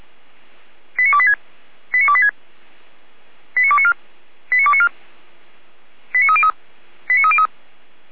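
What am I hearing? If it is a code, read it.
Five-tone selective call (ZVEI), heard as quick runs of five tones, tones of about 70 ms. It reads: 89286 89286 89294 89294 89303 89303
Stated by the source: ZVEI five-tone selective call received by SDR.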